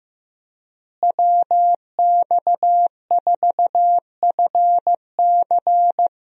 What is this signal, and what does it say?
Morse code, heard as a keyed tone, about 15 wpm, 700 Hz. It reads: WX4FC
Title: USA Call Signs - 5 Characters - Repeated 3x - 15wpm